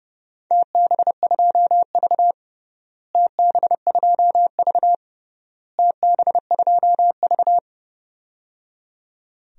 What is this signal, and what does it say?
Morse code, heard as a keyed tone, about 30 wpm, 700 Hz. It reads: TB2V TB2V TB2V
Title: DXCC-100 Call Signs - 4 Characters - Repeated 3x - 30wpm